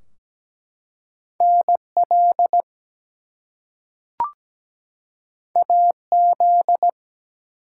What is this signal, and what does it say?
Morse code, heard as a keyed tone, about 17 wpm, 700 Hz. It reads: NL AZ
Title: US States and Canadian Provinces 17wpm